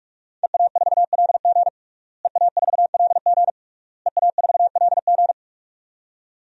Morse code, heard as a keyed tone, about 46 wpm, 700 Hz. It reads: EA4LG EA4LG EA4LG
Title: DXCC-100 Call Signs - 5 Characters - Repeated 3x - 45wpm